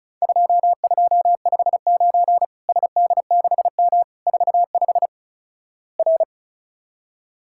Morse code, heard as a keyed tone, about 34 wpm, 700 Hz. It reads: 2259 SD6M 45 R
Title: Worked All Europe DX Contest - QTC Exchange - 35wpm